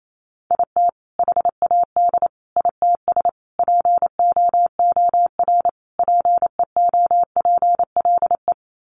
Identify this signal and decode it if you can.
Morse code, heard as a keyed tone, about 28 wpm, 700 Hz. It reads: IT HAD ITS POOR PEOPLE
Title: Sentences from Top 1000 Words 28wpm